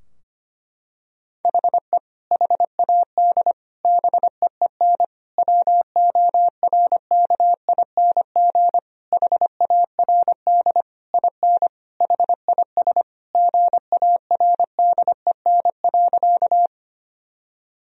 Morse code, heard as a keyed tone, about 25 wpm, 700 Hz. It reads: HE HAD BEEN WORKING HARD IN HIS GARDEN.
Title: Sentences from Top 1000 Words - Review Part 4 - 25wpm